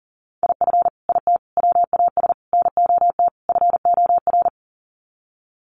Morse code, heard as a keyed tone, about 37 wpm, 700 Hz. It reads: IF IT WAS NOT FOR
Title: Sets of 5 Words Encore 40wpm